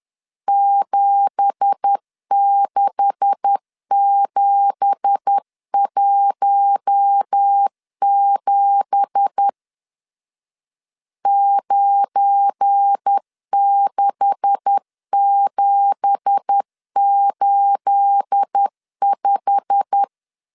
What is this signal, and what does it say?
Morse code, heard as a keyed tone, about 10 wpm, 800 Hz. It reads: 76717 96785
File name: LEZIONI / 9 2867 10wpm 2733